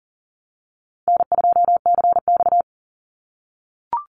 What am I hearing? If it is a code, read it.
Morse code, heard as a keyed tone, about 40 wpm, 700 Hz. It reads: N2CX
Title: Call Signs - 40wpm